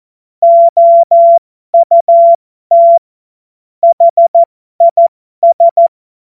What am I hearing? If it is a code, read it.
Morse code, heard as a keyed tone, about 14 wpm, spaced as for 12 wpm, 680 Hz. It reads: OUT HIS
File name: Morsekurs / Witze